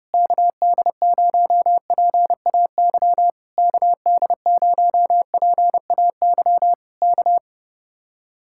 Morse code, heard as a keyed tone, about 30 wpm, 700 Hz. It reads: KD0PAY KD0PAY K